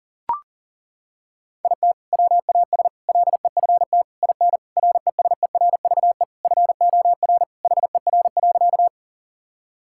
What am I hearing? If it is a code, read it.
Morse code, heard as a keyed tone, about 40 wpm, 700 Hz. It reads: IT WAS LEFT IN RESERVE FOR HER.
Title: Sentences from Top 1000 Words - Review Part 6 - 40wpm